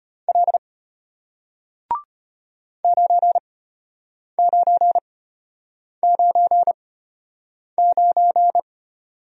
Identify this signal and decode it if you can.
Morse code, heard as a keyed tone, about 33 wpm, spaced as for 8 wpm, 690 Hz. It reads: L 9999